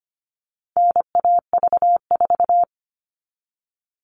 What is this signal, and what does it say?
Morse code, heard as a keyed tone, about 25 wpm, 700 Hz. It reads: NAV4
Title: All Cut Numbers - 4-Digit 25wpm